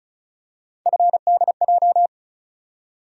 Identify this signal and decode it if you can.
Morse code, heard as a keyed tone, about 34 wpm, 700 Hz. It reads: FDJ
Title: USA Call Sign Postfixes - 3 Letters 35wpm